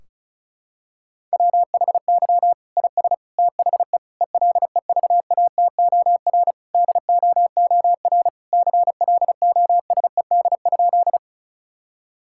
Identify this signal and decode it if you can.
Morse code, heard as a keyed tone, about 36 wpm, 700 Hz. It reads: WHY IS THE ELEVATOR DOOR CLOSED?
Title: Sentences from Top 1600 Words 35wpm